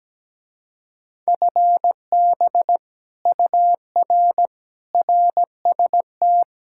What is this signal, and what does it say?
Morse code, heard as a keyed tone, about 17 wpm, 700 Hz. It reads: FB UR RST
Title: Speed-Racing Common QSO Phrases - 17wpm